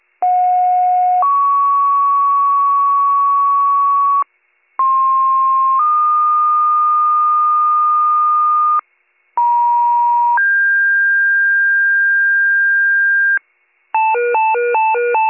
Two-tone page, heard as a one-second tone, then a three-second tone, about 730 then 1120 Hz, about 1060 then 1220 Hz, about 950 then 1600 Hz. It is heard from 0.2 to 13.4 s.